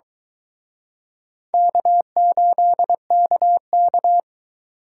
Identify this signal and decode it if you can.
Morse code, heard as a keyed tone, about 23 wpm, 700 Hz. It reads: K8KK